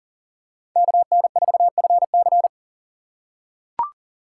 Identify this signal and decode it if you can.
Morse code, heard as a keyed tone, about 40 wpm, 690 Hz. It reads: KN4FC
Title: USA Call Signs - 2x Character Spacing - 40wpm